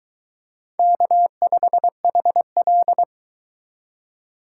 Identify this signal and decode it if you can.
Morse code, heard as a keyed tone, about 22 wpm, 700 Hz. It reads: K5HL